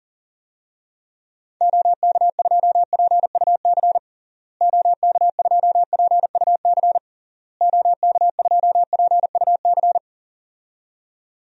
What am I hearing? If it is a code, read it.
Morse code, heard as a keyed tone, about 37 wpm, 700 Hz. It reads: OK2PUC OK2PUC OK2PUC
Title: DXCC-100 Call Signs - 6 Characters - Repeated 3x - 40wpm